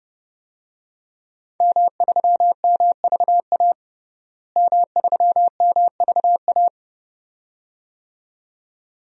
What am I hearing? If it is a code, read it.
Morse code, heard as a keyed tone, about 32 wpm, 700 Hz. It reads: M3MVA M3MVA
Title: DXCC-100 Call Signs - 5 Characters - Repeated 2x - 30wpm